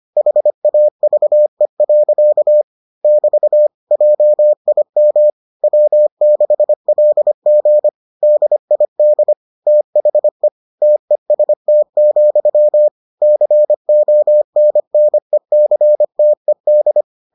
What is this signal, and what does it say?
Morse code, heard as a keyed tone, about 25 wpm, 600 Hz. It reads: HAVE. <BT> JIM W6LG DID THE TEST, CONNECTED